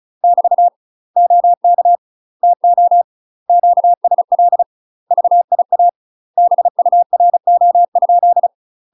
Morse code, heard as a keyed tone, about 36 wpm, 700 Hz. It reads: <BT> OK TO QSL VIA BURO?
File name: UK MASTER QSOs TRAINING / 35WPM / UK MASTER QSOs 0032